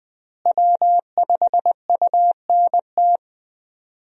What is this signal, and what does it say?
Morse code, heard as a keyed tone, about 20 wpm, 700 Hz. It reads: W5UNT
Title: USA Call Signs - 5 Characters - Repeated 3x - 20wpm